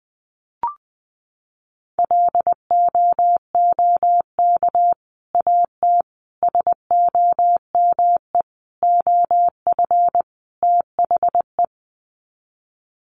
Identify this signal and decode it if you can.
Morse code, heard as a keyed tone, about 20 wpm, 700 Hz. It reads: LOOK AT SOME OF THE
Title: Sets of 5 Words Encore 20wpm